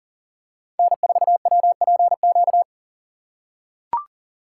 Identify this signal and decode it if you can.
Morse code, heard as a keyed tone, about 37 wpm, 700 Hz. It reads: N4WPQ
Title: USA Call Signs - 5 Characters - 2x Character Spacing - 40wpm